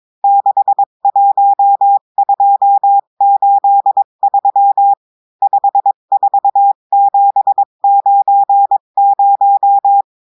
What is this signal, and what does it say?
Morse code, heard as a keyed tone, about 22 wpm, 800 Hz. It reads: 61283 54790
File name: cours-du-mercredi / audios / cours-853-du-2-mai-2024-etxte chiffre et texte clair-a-20-wpm